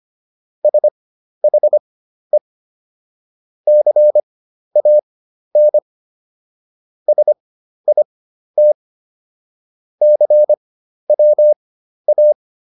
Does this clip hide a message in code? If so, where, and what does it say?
Yes: Morse code from 0.6 to 12.3 s, heard as a keyed tone, about 25 wpm, spaced as for 12 wpm, 600 Hz. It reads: SHE CAN SIT CWA